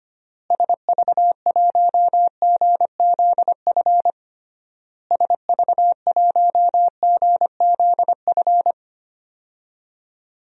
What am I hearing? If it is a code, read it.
Morse code, heard as a keyed tone, about 25 wpm, 700 Hz. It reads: SV1GZF SV1GZF
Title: DXCC-100 Call Signs - 6 Characters - Repeated 3x - 25wpm